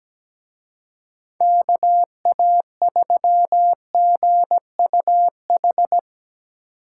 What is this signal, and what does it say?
Morse code, heard as a keyed tone, about 19 wpm, 700 Hz. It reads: KA3GUH
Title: Speed Racing - USA 2x3 Call Signs 17wpm